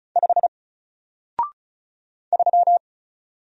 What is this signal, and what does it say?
Morse code, heard as a keyed tone, about 34 wpm, 700 Hz. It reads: 5 3
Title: Single Letter-Number Alt 35wpm